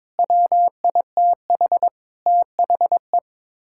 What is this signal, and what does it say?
Morse code, heard as a keyed tone, about 22 wpm, 700 Hz. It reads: WITH THE
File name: practice / 22wpm / Sets of 6 Words - 22wpm